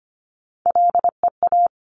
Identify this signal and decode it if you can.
Morse code, heard as a keyed tone, about 25 wpm, 700 Hz. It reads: LEA
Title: Top Three Letter Combinations in Words 25wpm